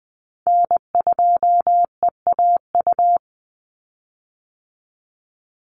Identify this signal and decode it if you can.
Morse code, heard as a keyed tone, about 20 wpm, 700 Hz. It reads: N2EAU
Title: USA Call Signs 20wpm